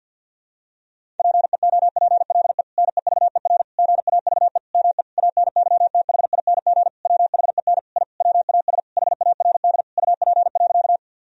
Morse code, heard as a keyed tone, about 50 wpm, 700 Hz. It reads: PEOPLE NEVER GAVE ME ANYTHING WHEN I WAS HARD UP.
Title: Sentences from Top 1000 Words - Review Part 1 - 50wpm